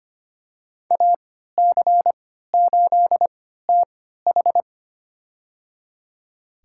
Morse code, heard as a keyed tone, about 25 wpm, spaced as for 14 wpm, 700 Hz. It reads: AC8TH
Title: USA Call Signs - 5 Characters - 3x Character Spacing - 25wpm